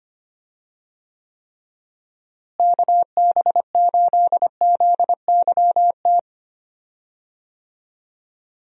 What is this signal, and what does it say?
Morse code, heard as a keyed tone, about 25 wpm, 700 Hz. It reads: KB8ZYT
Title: USA Call Signs - 6 Characters - Repeated 3x - 25wpm